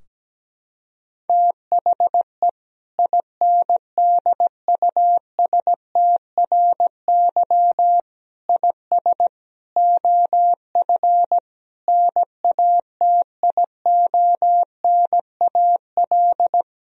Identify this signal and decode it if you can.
Morse code, heard as a keyed tone, about 17 wpm, 700 Hz. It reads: THE INDUSTRY IS OF NATIONAL
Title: Sentences from Top 1000 Words - Review Part 4 - 17wpm